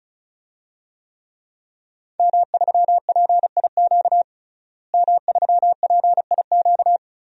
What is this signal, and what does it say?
Morse code, heard as a keyed tone, about 34 wpm, 700 Hz. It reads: M3PIQ M3PIQ
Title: DXCC-100 Call Signs - 5 Characters - Repeated 2x - 35wpm